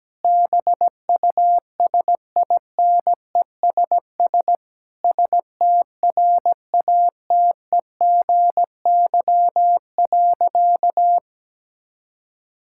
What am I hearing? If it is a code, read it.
Morse code, heard as a keyed tone, about 17 wpm, 700 Hz. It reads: BUSINESS STRATEGY.
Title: Sentences from Top 1000 Words - Review Part 5 - 17wpm